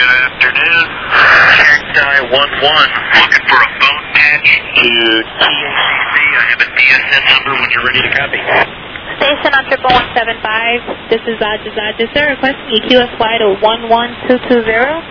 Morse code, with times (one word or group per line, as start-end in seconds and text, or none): none